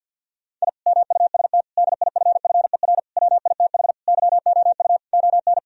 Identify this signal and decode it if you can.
Morse code, heard as a keyed tone, about 50 wpm, 700 Hz. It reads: I MUST DIFFER WITH YOU ON
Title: Sentences from Top 1000 Words - Review Part 1 - 50wpm